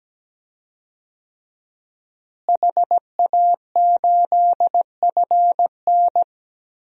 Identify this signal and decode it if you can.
Morse code, heard as a keyed tone, about 17 wpm, 700 Hz. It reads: HA8FN
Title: DXCC-100 Call Signs - 5 Characters - Repeated 3x - 17wpm